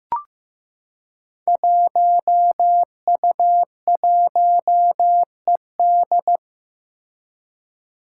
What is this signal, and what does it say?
Morse code, heard as a keyed tone, about 15 wpm, 700 Hz. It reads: 1U1ED
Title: Five Challenging Letter-Number Combinations 15wpm